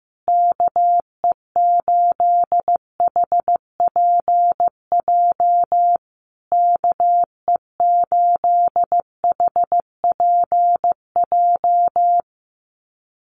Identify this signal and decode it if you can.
Morse code, heard as a keyed tone, about 15 wpm, 700 Hz. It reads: KE8HPJ KE8HPJ
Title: USA Call Signs Repeated 3x 15wpm